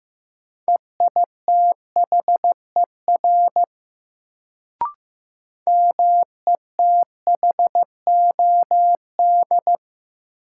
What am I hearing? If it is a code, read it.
Morse code, heard as a keyed tone, about 15 wpm, 700 Hz. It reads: EITHER METHOD
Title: Sentences from Top 600 Words - New Word Drill - 15wpm